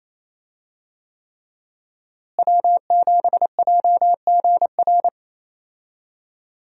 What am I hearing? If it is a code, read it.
Morse code, heard as a keyed tone, about 28 wpm, 700 Hz. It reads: W7JGR
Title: USA Call Signs - 5 Characters - Repeated 3x - 28wpm